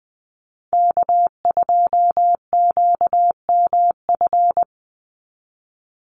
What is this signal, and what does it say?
Morse code, heard as a keyed tone, about 20 wpm, 700 Hz. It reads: K2QMF
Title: Call Signs Repeated 2x - 20wpm